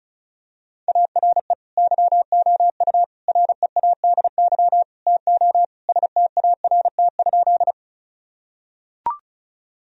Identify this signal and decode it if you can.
Morse code, heard as a keyed tone, about 34 wpm, 700 Hz. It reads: ARE YOU READY TO START?
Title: Sentences from Top 1000 Words - Review Part 1 - 35wpm